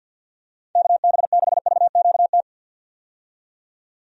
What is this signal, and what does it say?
Morse code, heard as a keyed tone, about 53 wpm, 700 Hz. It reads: KB6VQT
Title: USA Call Signs - 6 Characters - 50wpm